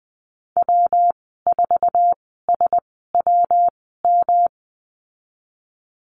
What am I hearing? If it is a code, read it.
Morse code, heard as a keyed tone, about 20 wpm, spaced as for 15 wpm, 700 Hz. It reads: W4SWM
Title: 122 Lesson 20g - Number 3 - Call Signs 20wpm